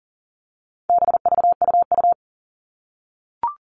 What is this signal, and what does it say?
Morse code, heard as a keyed tone, about 40 wpm, 700 Hz. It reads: BVUU
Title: All Cut Numbers - 4-Digit 40wpm